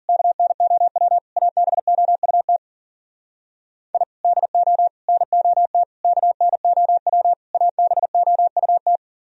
Morse code, heard as a keyed tone, about 52 wpm, 700 Hz. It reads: KNOW ABOUT I DO NOT KNOW ABOUT